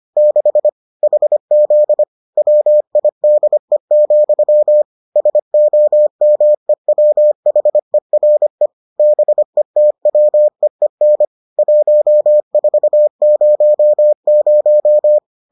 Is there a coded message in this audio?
Yes: Morse code from 0.2 to 15.2 s, heard as a keyed tone, about 25 wpm, 600 Hz. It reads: 6 HZ WIDE, SOMEWHERE BETWEEN 1400